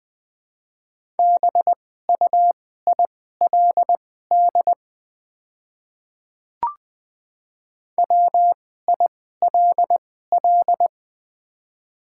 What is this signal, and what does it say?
Morse code, heard as a keyed tone, about 20 wpm, spaced as for 15 wpm, 700 Hz. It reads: BUILD WILL
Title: 2x Character Spacing - Top 500 Words - 20wpm